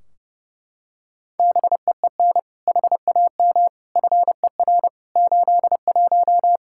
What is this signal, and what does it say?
Morse code, heard as a keyed tone, about 30 wpm, 700 Hz. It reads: BEEN HAM FER 81
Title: Practice QSO - Exchange 3 - More Info - 30wpm